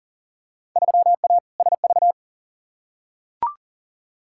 Morse code, heard as a keyed tone, about 40 wpm, 700 Hz. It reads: 3A SV
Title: Field Day Exchanges - 40wpm